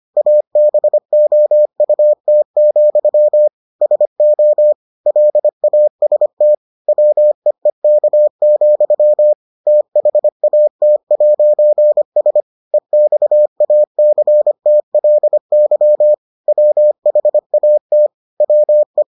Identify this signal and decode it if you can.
Morse code, heard as a keyed tone, about 25 wpm, 600 Hz. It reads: ABOUT, SO LAST WEEK, THAT'S EXACTLY WHAT WE